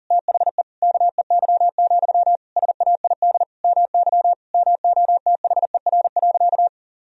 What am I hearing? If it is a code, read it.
Morse code, heard as a keyed tone, about 40 wpm, 700 Hz. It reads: THE KEY, SAID MY MOTHER.
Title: Sentences from Top 1000 Words - Review Part 4 - 40wpm